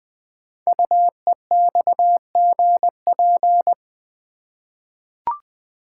Morse code, heard as a keyed tone, about 20 wpm, 700 Hz. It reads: UEXGP
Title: Code Groups - 5 Letters - 20wpm